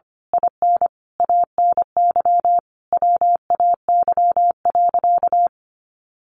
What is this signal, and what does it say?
Morse code, heard as a keyed tone, about 25 wpm, 700 Hz. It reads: IN ANY WAY.
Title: A Sound Of Thunder 25wpm